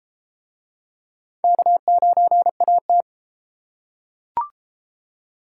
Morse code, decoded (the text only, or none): K9AT